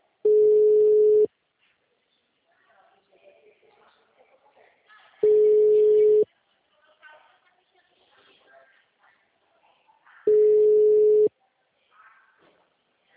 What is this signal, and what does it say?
Ringback tone (Europe, 425 Hz), with speech.